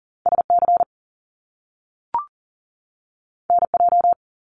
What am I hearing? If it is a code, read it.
Morse code, heard as a keyed tone, about 38 wpm, 700 Hz. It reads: SC NJ
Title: US States 40wpm